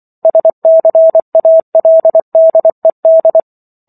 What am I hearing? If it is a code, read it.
Morse code, heard as a keyed tone, about 24 wpm, 650 Hz. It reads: SCALDED